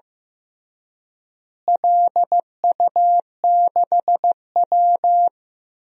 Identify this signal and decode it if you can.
Morse code, heard as a keyed tone, about 15 wpm, 700 Hz. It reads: LU6W
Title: DXCC-100 Call Signs - 4 Characters - Repeated 3x - 15wpm